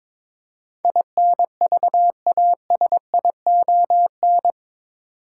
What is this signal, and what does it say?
Morse code, heard as a keyed tone, about 21 wpm, 700 Hz. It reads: INVASION
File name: practice / 22wpm / Eight Letter Words - 22wpm